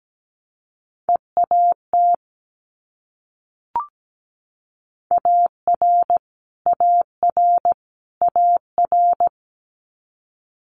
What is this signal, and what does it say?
Morse code, heard as a keyed tone, about 17 wpm, 700 Hz. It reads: EAT AR AR AR
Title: Multi-letter Phonemes Repeated 3x 17wpm